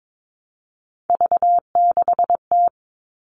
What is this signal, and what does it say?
Morse code, heard as a keyed tone, about 22 wpm, 700 Hz. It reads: V6T
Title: All Cut Numbers - 3-Digit 22wpm